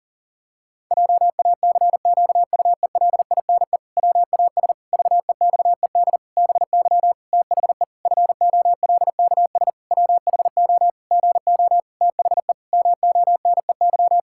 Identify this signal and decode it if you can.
Morse code, heard as a keyed tone, about 40 wpm, 700 Hz. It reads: JACQUELINE WAS VEXED BY THE FOLKS WHO GO THE MONEY
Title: Pangram Challenge v2 - 40wpm